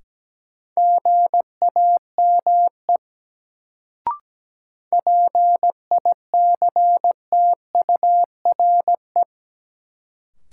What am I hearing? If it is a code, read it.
Morse code, heard as a keyed tone, about 17 wpm, 700 Hz. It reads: GAME PICTURE
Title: Sentences from Top 500 Words - New Word Drill - 17wpm